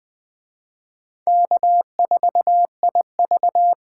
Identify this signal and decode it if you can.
Morse code, heard as a keyed tone, about 19 wpm, 700 Hz. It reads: K4IV